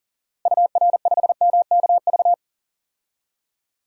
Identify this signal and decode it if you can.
Morse code, heard as a keyed tone, about 40 wpm, 700 Hz. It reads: UR5MKV
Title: DXCC-100 Call Signs - 6 Characters - Repeated 2x - 40wpm